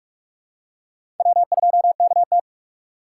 Morse code, heard as a keyed tone, about 43 wpm, 700 Hz. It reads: W2KT
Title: USA Call Signs - 6x Character Spacing - 45wpm